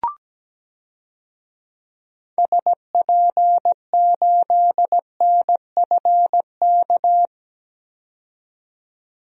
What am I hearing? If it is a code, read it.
Morse code, heard as a keyed tone, about 17 wpm, 700 Hz. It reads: SP8NFK
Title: DXCC-100 Call Signs - 6 Characters - 17wpm